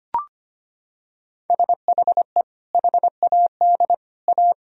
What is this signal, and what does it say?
Morse code, heard as a keyed tone, about 25 wpm, 700 Hz. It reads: SHE HAD A
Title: Sentences from Top 1000 Words - Review Part 4 - 25wpm